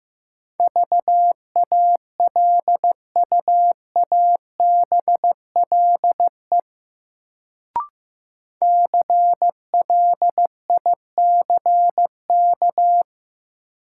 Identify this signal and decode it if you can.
Morse code, heard as a keyed tone, about 16 wpm, 700 Hz. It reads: VALUABLE CLICK
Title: Sentences from Top 1100 Words - New Word Drill - 15wpm